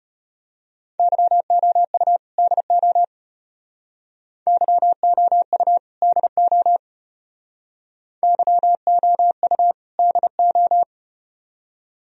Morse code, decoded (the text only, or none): YOU DO YOU DO YOU DO